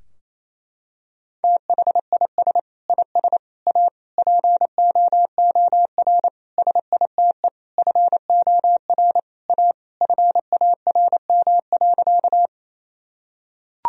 Morse code, heard as a keyed tone, about 27 wpm, 700 Hz. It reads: THIS IS A POOR SITE FOR A FARM.
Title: Sentences from Top 1000 Words - Review Part 4 - 28wpm